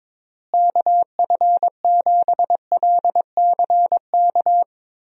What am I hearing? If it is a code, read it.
Morse code, heard as a keyed tone, about 22 wpm, 700 Hz. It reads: KF7LCK
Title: USA Call Signs - 6 Characters - Repeated 3x - 22wpm